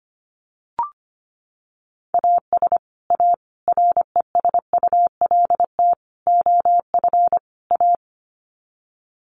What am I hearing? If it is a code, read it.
Morse code, heard as a keyed tone, about 25 wpm, 700 Hz. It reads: AS A RESULT OF A
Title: Sets of 5 Words Encore 25wpm